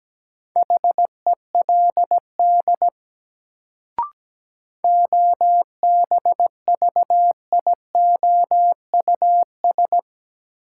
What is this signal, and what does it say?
Morse code, heard as a keyed tone, about 17 wpm, 700 Hz. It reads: HELD OBVIOUS
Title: Sentences from Top 1600 Words - New Word Drill - 17wpm